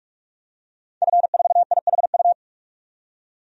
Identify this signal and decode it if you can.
Morse code, heard as a keyed tone, about 45 wpm, 700 Hz. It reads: F4IHU
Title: DXCC-100 Call Signs - 5 Characters - 45wpm